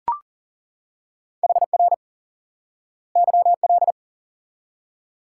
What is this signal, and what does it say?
Morse code, heard as a keyed tone, about 40 wpm, 700 Hz. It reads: HR YL
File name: IQE Course 3 - Lesson 89 - YAGI - Two Elements - 40wpm